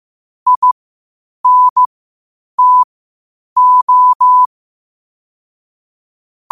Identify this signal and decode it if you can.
Morse code, heard as a keyed tone, about 15 wpm, spaced as for 8 wpm, 1000 Hz. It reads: INTO
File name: qotdfiles / 8wpm